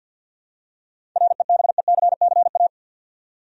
Morse code, heard as a keyed tone, about 53 wpm, 700 Hz. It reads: REBECCA
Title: USA Common Names - 50wpm